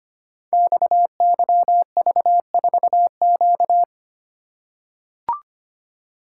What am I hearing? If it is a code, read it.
Morse code, heard as a keyed tone, about 25 wpm, 700 Hz. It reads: XYV4Q
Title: Code Groups - 5 Letter:Numbers - 25wpm